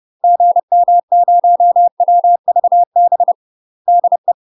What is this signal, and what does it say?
Morse code, heard as a keyed tone, about 30 wpm, 700 Hz. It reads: GM0WVB DE
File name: UK MASTER QSOs TRAINING / 30WPM / UK MASTER QSOs 0040